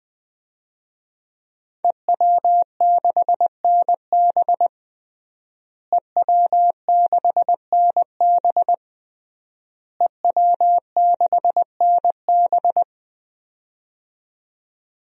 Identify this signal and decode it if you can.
Morse code, heard as a keyed tone, about 20 wpm, 700 Hz. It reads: EW6NB EW6NB EW6NB
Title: DXCC-100 Call Signs - 5 Characters - Repeated 3x - 20wpm